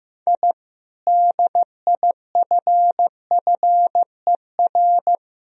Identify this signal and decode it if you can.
Morse code, heard as a keyed tone, about 15 wpm, 700 Hz. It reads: I DIFFER
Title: Sentences from Top 1000 Words - Review Part 1 - 15wpm